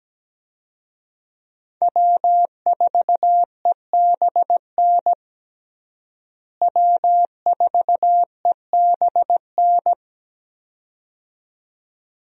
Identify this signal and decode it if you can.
Morse code, heard as a keyed tone, about 17 wpm, 700 Hz. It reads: W4EBN W4EBN
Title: USA Call Signs - 5 Characters - Repeated 2x - 17wpm